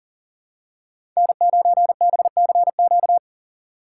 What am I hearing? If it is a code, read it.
Morse code, heard as a keyed tone, about 40 wpm, 700 Hz. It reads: N9BCQ
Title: USA Call Signs - 5 Characters - 4x Character Spacing - 40wpm